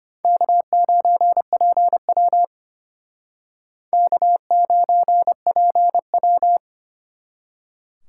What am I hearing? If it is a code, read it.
Morse code, heard as a keyed tone, about 32 wpm, 700 Hz. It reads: K9PW K9PW